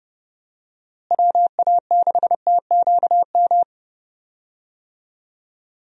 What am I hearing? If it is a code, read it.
Morse code, heard as a keyed tone, about 30 wpm, 700 Hz. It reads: WA6TQM
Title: USA Call Signs - 6 Characters - Repeated 2x - 30wpm